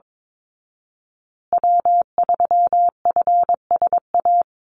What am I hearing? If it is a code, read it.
Morse code, heard as a keyed tone, about 22 wpm, 700 Hz. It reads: W3FSA
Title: Call Signs Repeated 2x - 22wpm